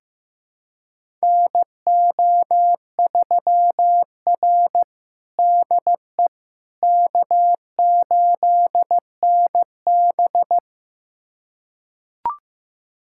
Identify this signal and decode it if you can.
Morse code, heard as a keyed tone, about 16 wpm, 700 Hz. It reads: NO3R DE K8NB
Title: USA Callsign DE Callsign - 4 Characters - 15wpm